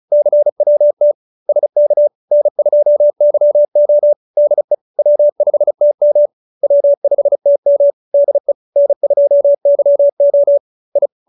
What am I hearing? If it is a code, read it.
Morse code, heard as a keyed tone, about 35 wpm, 590 Hz. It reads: CWT SK N2YO DE W5TM W5TM DE N2YO I